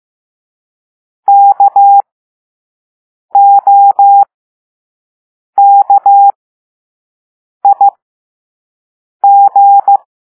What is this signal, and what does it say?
Morse code, heard as a keyed tone, about 15 wpm, spaced as for 5 wpm, 800 Hz. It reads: KOKIG
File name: Morse / Level5-All-B